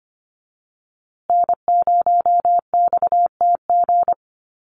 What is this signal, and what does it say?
Morse code, heard as a keyed tone, about 25 wpm, 700 Hz. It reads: N0XTG
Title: USA Call Signs Repeated 3x 25wpm